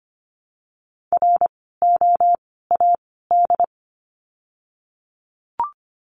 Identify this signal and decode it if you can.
Morse code, heard as a keyed tone, about 24 wpm, 700 Hz. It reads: R O A D
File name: course / 25:10wpm / 094 Lesson 16f - Letter c - Words 25wpm